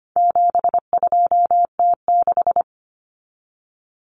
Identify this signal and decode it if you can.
Morse code, heard as a keyed tone, about 25 wpm, 700 Hz. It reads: 72T6
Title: Common Cut Numbers - 4-Digit 25wpm